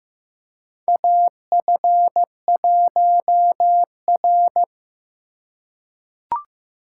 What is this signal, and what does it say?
Morse code, heard as a keyed tone, about 15 wpm, 700 Hz. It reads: AF1R
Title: USA Call Signs - 4 Characters - 2x Character Spacing - 15wpm